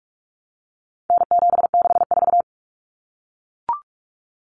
Morse code, heard as a keyed tone, about 44 wpm, 700 Hz. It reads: N764
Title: Common Cut Numbers - 4-Digit 45wpm